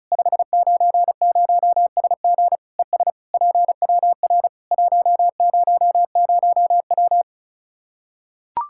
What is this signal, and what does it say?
Morse code, heard as a keyed tone, about 34 wpm, 700 Hz. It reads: 590SG ES PWR 100W
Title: Practice QSO - Exchange 2 - Info - 35wpm